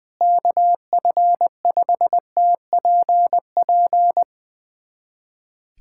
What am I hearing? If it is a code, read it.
Morse code, heard as a keyed tone, about 20 wpm, 700 Hz. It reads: KF5TPP
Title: USA Callsign DE Callsign - 6 Characters - 20wpm